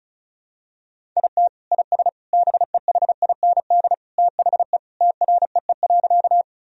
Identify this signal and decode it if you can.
Morse code, heard as a keyed tone, about 36 wpm, 700 Hz. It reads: IT IS BEHIND THE TREE.
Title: Sentences from Top 1000 Words - Review Part 1 - 35wpm